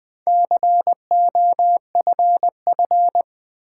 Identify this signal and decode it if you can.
Morse code, heard as a keyed tone, about 20 wpm, 700 Hz. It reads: COFF
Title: Build A Word - 2 Letters At A Time - 6-Letter Words - 20wpm